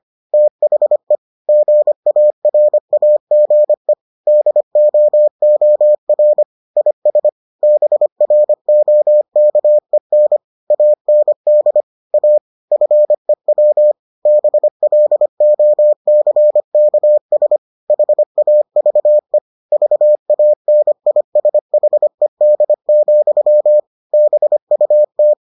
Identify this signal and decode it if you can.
Morse code, heard as a keyed tone, about 25 wpm, 600 Hz. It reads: THE GARAGE DOOR IS BROKEN AND A FEW BLOCKS HAVE VANISHED, BUT